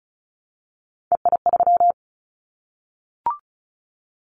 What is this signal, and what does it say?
Morse code, heard as a keyed tone, about 36 wpm, 700 Hz. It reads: EI3
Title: Callsign Prefixes for Top 50 DXCC - 35wpm